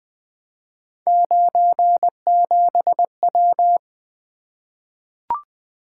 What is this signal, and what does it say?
Morse code, heard as a keyed tone, about 20 wpm, 700 Hz. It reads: 97W
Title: Three Letter-Number Combinations - 20wpm